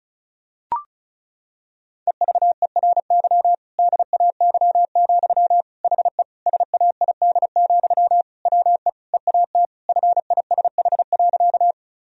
Morse code, heard as a keyed tone, about 36 wpm, 700 Hz. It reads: EVERY DAY, HE SAID, WE EAT FISH.
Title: Sentences from Top 1000 Words - Review Part 1 - 35wpm